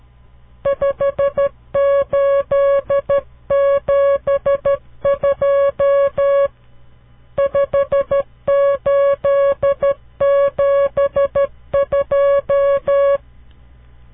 Morse code, heard as a keyed tone, about 12 wpm, 560 Hz. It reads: 5872 5872
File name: CW Learning Programs / farnsworth / unit 5